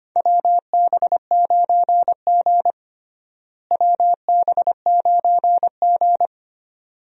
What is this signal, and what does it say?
Morse code, heard as a keyed tone, about 25 wpm, 700 Hz. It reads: WB9G WB9G
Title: USA Call Signs - 4 Characters - Repeated 2x - 25wpm